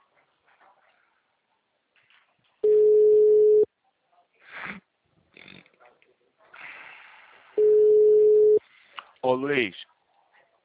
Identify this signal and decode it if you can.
Ringback tone (Europe, 425 Hz), with speech.
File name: out- -2005- - - .14275